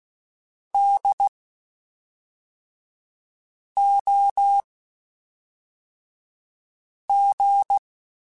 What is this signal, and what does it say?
Morse code, heard as a keyed tone, about 16 wpm, spaced as for 3 wpm, 800 Hz. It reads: DOG
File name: General / General-mp3s / Houses @5